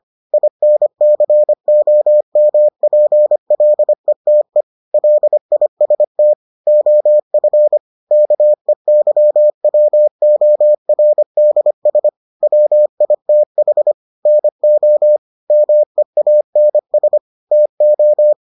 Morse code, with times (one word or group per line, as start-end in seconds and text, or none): 0.3-4.6 INCOMPLETE
4.9-6.3 LIST
6.7-7.8 OF
8.1-12.1 KEYWORDS
12.4-13.9 WITH
14.3-15.2 NO
15.5-17.2 MEANS
17.5-18.3 TO